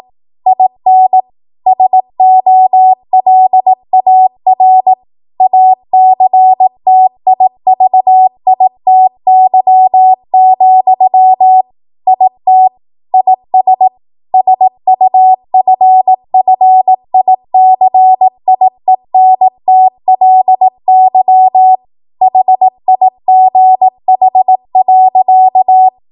Morse code, heard as a keyed tone, about 18 wpm, 750 Hz. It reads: IN SOLAR ACTIVITY, IT IS SUFFICIENTLY HIGH.